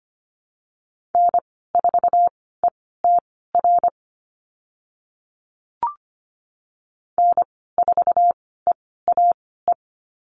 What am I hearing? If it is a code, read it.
Morse code, heard as a keyed tone, about 25 wpm, spaced as for 16 wpm, 700 Hz. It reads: N4ETR N4EAE
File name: Lesson 15g - Number 5 - Call Signs 25wpm